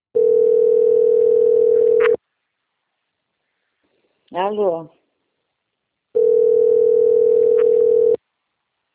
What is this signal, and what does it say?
Ringback tone (North America), 440 + 480 Hz, with speech.
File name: out- - - - - .36293